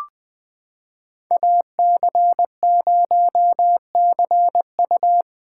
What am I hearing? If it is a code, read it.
Morse code, heard as a keyed tone, about 21 wpm, 700 Hz. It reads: AC0CU
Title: 2023 POTA Hunter Call Signs - 20wpm